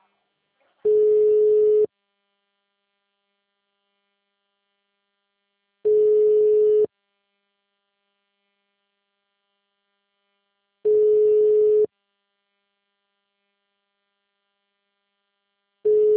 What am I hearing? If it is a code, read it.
Ringback tone (Europe, 425 Hz).
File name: out- -2008- - - .980